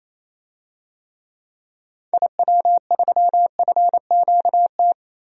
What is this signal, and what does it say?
Morse code, heard as a keyed tone, about 27 wpm, 700 Hz. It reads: IW3FQT
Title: DXCC-100 Call Signs - 6 Characters - Repeated 3x - 28wpm